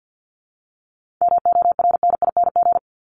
Morse code, heard as a keyed tone, about 52 wpm, 700 Hz. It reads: MORNING
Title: Top 500 Words 50wpm